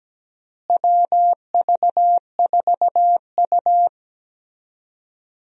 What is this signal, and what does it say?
Morse code, heard as a keyed tone, about 17 wpm, 700 Hz. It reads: WV4U
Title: USA Call Signs - 4 Characters - Repeated 3x - 17wpm